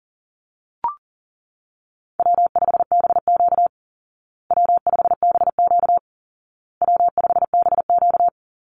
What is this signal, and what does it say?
Morse code, heard as a keyed tone, about 40 wpm, 700 Hz. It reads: W5BQ W5BQ W5BQ
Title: Easy Call Signs Repeated 3x - 40wpm